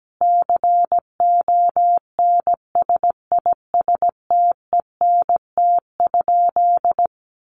Morse code, heard as a keyed tone, about 17 wpm, 700 Hz. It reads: CONSISTENT?
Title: Sentences from Top 1500 Words 17wpm